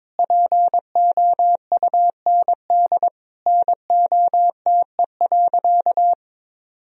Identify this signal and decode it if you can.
Morse code, heard as a keyed tone, about 22 wpm, 700 Hz. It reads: POUND NOTE.
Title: Sentences from Top 1000 Words - Review Part 2 - 22wpm